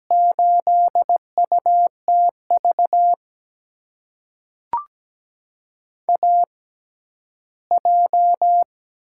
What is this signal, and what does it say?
Morse code, heard as a keyed tone, about 17 wpm, spaced as for 11 wpm, 700 Hz. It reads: 8UTV AJ
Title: USA Call Signs - 6x Character Spacing - 17wpm